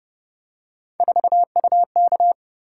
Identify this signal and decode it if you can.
Morse code, heard as a keyed tone, about 31 wpm, 700 Hz. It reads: 4UK
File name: Speed-Racing - Three Letter-Number Combinations - 25wpm